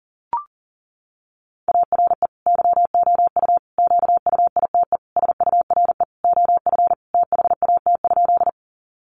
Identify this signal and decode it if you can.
Morse code, heard as a keyed tone, about 40 wpm, 700 Hz. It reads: ARE YOU QUITE SURE OF THAT?
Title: Sentences from Top 700 Words 40wpm